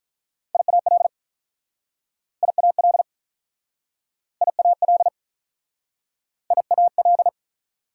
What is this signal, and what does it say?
Morse code, heard as a keyed tone, about 48 wpm, 700 Hz. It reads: IAL IAL IAL IAL